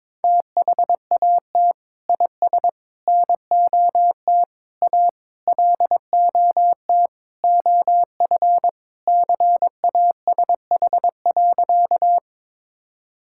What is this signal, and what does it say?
Morse code, heard as a keyed tone, about 22 wpm, 700 Hz. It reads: THAT IS NOT A LOT OF CASH.
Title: Sentences from Top 1000 Words - Review Part 5 - 22wpm